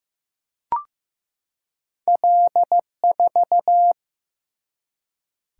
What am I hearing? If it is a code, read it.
Morse code, heard as a keyed tone, about 15 wpm, 700 Hz. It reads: L4
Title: Single-Letter Single-Number 15wpm